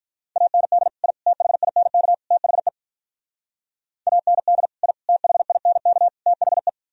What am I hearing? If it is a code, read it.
Morse code, heard as a keyed tone, about 46 wpm, 700 Hz. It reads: AND I THINK THE AND I THINK THE